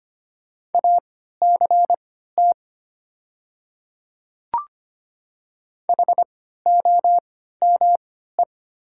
Morse code, heard as a keyed tone, about 25 wpm, spaced as for 14 wpm, 700 Hz. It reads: ACT HOME
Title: 3x Character Spacing - Top 200 Words - 25wpm